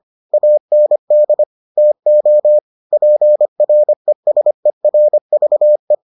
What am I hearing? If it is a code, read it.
Morse code, heard as a keyed tone, about 25 wpm, 600 Hz. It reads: AND TO PRESERVE